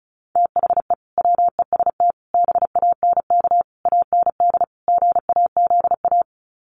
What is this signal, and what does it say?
Morse code, heard as a keyed tone, about 34 wpm, 700 Hz. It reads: THE WEST BANK AND GAZA
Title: Sets of 5 Words 35wpm